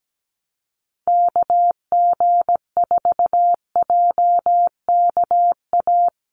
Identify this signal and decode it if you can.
Morse code, heard as a keyed tone, about 17 wpm, 700 Hz. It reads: KG4JKA
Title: USA Call Signs Repeated 2x 17wpm